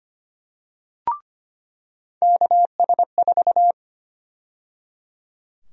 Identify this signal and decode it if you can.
Morse code, heard as a keyed tone, about 25 wpm, 700 Hz. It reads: KS4
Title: USA Call Sign Prefixes - 25wpm